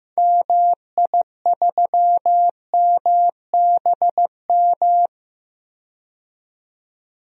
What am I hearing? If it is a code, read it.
Morse code, heard as a keyed tone, about 15 wpm, 700 Hz. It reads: MI3MBM
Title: DXCC-100 Call Signs - 6 Characters - 15wpm